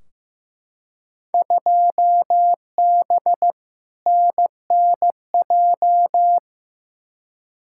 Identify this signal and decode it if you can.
Morse code, heard as a keyed tone, about 15 wpm, 700 Hz. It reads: 2B NNJ
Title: Field Day Exchanges Encore - 15wpm